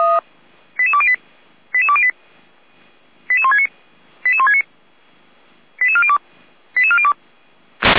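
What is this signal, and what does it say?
Five-tone selective call (ZVEI), heard as quick runs of five tones, tones of about 80 ms. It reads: 88208 88208 88169 88169 88402 88402
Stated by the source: Five-tone ZVEI selective call received on an SDR.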